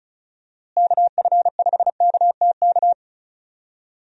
Morse code, heard as a keyed tone, about 36 wpm, 700 Hz. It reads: KF5KTK
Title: USA Call Signs - 6 Characters - Repeated 3x - 35wpm